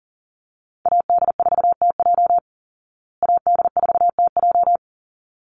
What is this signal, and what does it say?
Morse code, heard as a keyed tone, about 40 wpm, 700 Hz. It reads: AD4TJ AD4TJ
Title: Call Signs Repeated 3x - 40wpm